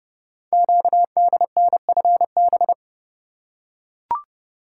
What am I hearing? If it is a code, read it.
Morse code, heard as a keyed tone, about 30 wpm, 700 Hz. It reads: QDNFB
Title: Code Groups - 5 Letters - 30wpm